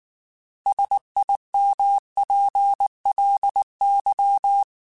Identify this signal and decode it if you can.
Morse code, heard as a keyed tone, about 19 wpm, 800 Hz. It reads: SIMPLY